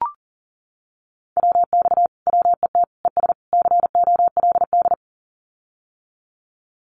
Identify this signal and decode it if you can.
Morse code, heard as a keyed tone, about 40 wpm, 700 Hz. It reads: WX WET ES COLD
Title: Common QSO Phrases Encore 40wpm